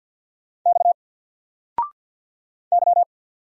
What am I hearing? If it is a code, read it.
Morse code, heard as a keyed tone, about 50 wpm, 700 Hz. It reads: X Y
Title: Single Letter-Number Alt 50wpm